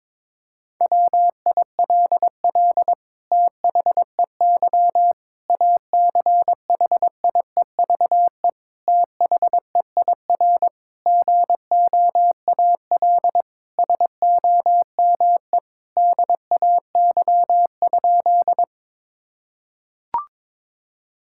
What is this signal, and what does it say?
Morse code, heard as a keyed tone, about 22 wpm, 700 Hz. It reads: WILL THEY ACHIEVE THEIR GOAL SOME DAY?
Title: Sentences from Top 1000 Words - Review Part 6 - 22wpm